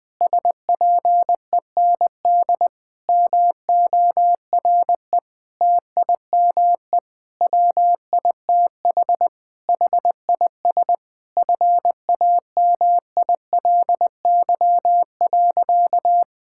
Morse code, heard as a keyed tone, about 21 wpm, 700 Hz. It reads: SPEND MORE TIME WITH HIS FAMILY.
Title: Sentences with 7-Word Phrases - 20wpm